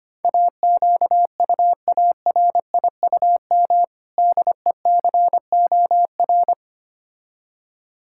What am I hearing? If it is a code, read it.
Morse code, heard as a keyed tone, about 25 wpm, 700 Hz. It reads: AQUARIUM DECOR
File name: practice / 25wpm / Difficult Words with Hints - 25wpm